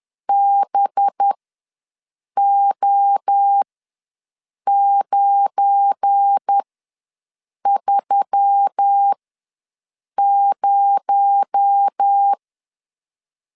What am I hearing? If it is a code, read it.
Morse code, heard as a keyed tone, about 10 wpm, spaced as for 6 wpm, 800 Hz. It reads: BO930
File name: LEZIONI / 7 GF9Z / 7 GF9 ws 10wpm 4034